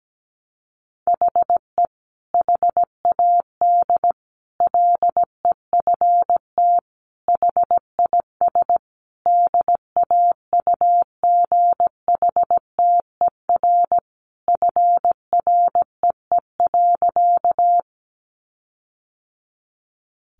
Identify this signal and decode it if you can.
Morse code, heard as a keyed tone, about 17 wpm, 700 Hz. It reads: HE HAD LEFT HIS DAUGHTER FREE.
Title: Sentences from Top 1500 Words 17wpm